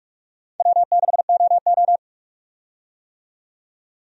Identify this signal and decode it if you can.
Morse code, heard as a keyed tone, about 45 wpm, 700 Hz. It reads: W6OO
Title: USA Call Signs - 4 Characters - 45wpm